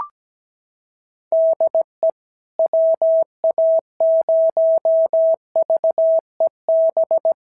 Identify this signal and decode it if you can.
Morse code, heard as a keyed tone, about 17 wpm, 690 Hz. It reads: DE WA0VEB
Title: SOTA QSO Practice 17wpm